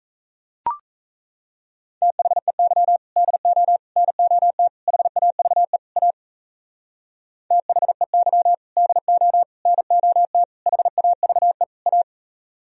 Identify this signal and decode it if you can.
Morse code, heard as a keyed tone, about 37 wpm, 700 Hz. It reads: THEY DO NOT HAVE A THEY DO NOT HAVE A